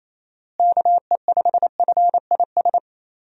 Morse code, heard as a keyed tone, about 28 wpm, 700 Hz. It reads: KE5FIS
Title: USA Call Signs - 6 Characters - Repeated 2x - 28wpm